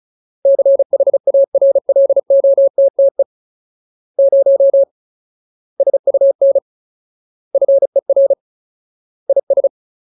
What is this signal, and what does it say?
Morse code, heard as a keyed tone, about 34 wpm, 550 Hz. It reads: CHARLOTTE 0 SUN FER IS
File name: Ultimate Word List / 35WPM Ultimate